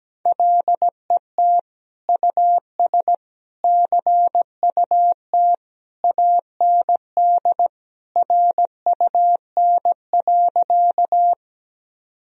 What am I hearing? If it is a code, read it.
Morse code, heard as a keyed tone, about 17 wpm, 700 Hz. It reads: LET US CUT AND RUN.
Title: Sentences from Top 1000 Words - Review Part 1 - 17wpm